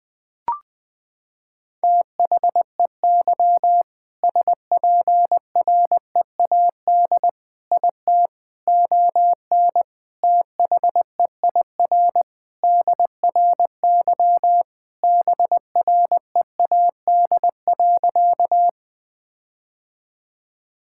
Morse code, heard as a keyed tone, about 20 wpm, 700 Hz. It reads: THEY SPREAD IT ON THEIR DRY BREAD.
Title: Sentences from Top 1600 Words 20wpm